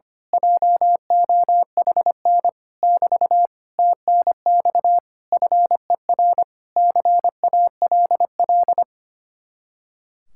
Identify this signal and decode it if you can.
Morse code, heard as a keyed tone, about 24 wpm, 700 Hz. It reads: JOHN <BT> TNX FER CALL
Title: Speed-Racing Common QSO Phrases - 25wpm